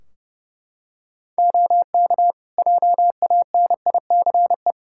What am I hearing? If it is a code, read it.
Morse code, heard as a keyed tone, about 30 wpm, 700 Hz. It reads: OK JANICE
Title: Practice QSO - Exchange 3 - More Info - 30wpm